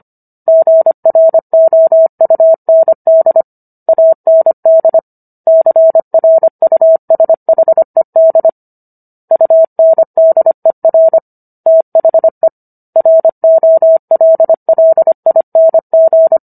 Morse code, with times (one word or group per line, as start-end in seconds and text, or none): 0.5-3.4 GROUND
3.9-5.0 AND
5.5-8.5 CRUSHED
9.3-11.2 UNDER
11.7-12.5 THE
13.0-16.4 ROLLING